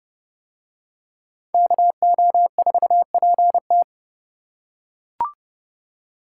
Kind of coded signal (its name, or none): Morse code